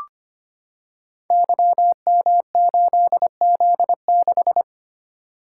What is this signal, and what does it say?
Morse code, heard as a keyed tone, about 25 wpm, 700 Hz. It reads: YM8Z6
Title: Code Groups - 5 Letter:Numbers - 25wpm